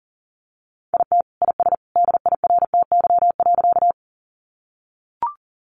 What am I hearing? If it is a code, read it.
Morse code, heard as a keyed tone, about 40 wpm, 700 Hz. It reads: IT IS DIRTY.
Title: Additional Easy Phrases 40wpm